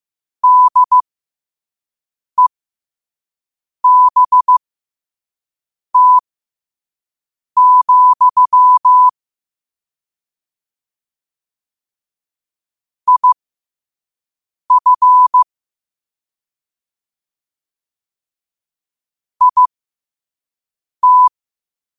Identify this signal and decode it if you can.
Morse code, heard as a keyed tone, about 15 wpm, spaced as for 5 wpm, 1000 Hz. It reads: DEBT, IF IT
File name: qotdfiles / 5wpm